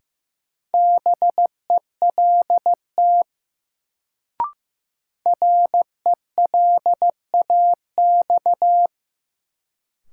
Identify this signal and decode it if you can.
Morse code, heard as a keyed tone, about 15 wpm, 700 Hz. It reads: BELT RELAX
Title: Sentences from Top 1400 Words - New Word Drill - 15wpm